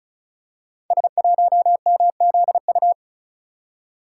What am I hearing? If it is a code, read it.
Morse code, heard as a keyed tone, about 36 wpm, 700 Hz. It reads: S1MZU
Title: Code Groups - 5 Letter:Numbers - 35wpm